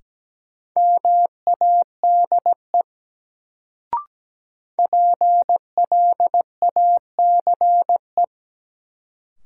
Morse code, heard as a keyed tone, about 17 wpm, 700 Hz. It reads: MADE PLACE
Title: Sentences from Top 200 Words - New Word Drill - 17wpm